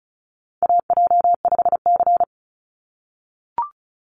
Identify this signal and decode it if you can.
Morse code, heard as a keyed tone, about 34 wpm, 700 Hz. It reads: AJ5C
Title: NA SOTA Chaser Callsigns 35wpm